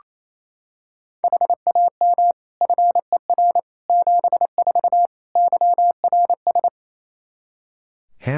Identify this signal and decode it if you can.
Morse code, heard as a keyed tone, about 28 wpm, 700 Hz. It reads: HAM FER 74 YRS
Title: Practice QSO - Exchange 3 - More Info - 28wpm